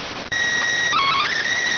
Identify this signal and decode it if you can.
SSTV (slow-scan television), PD 290.